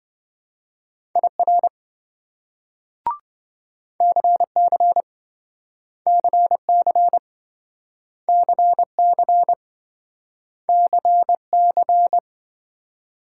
Morse code, heard as a keyed tone, about 26 wpm, 700 Hz. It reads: IR CC CC CC CC